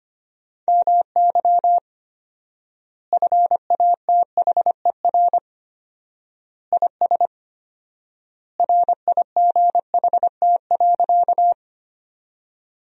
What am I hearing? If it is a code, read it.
Morse code, heard as a keyed tone, about 25 wpm, 700 Hz. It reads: MY FATHER IS RIGHT.
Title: 4x Word-Spacing - Sentences from Top 300 Words - 25wpm